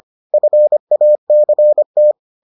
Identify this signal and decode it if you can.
Morse code, heard as a keyed tone, about 25 wpm, 600 Hz. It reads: FACT